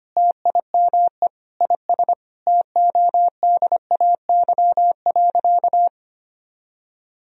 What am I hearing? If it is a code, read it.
Morse code, heard as a keyed tone, about 25 wpm, 700 Hz. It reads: TIME IS TODAY.
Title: Sentences from Top 1000 Words - Review Part 2 - 25wpm